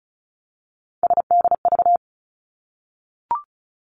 Morse code, heard as a keyed tone, about 36 wpm, 700 Hz. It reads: SDV
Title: Three Challenging Letter-Number Combinations 35wpm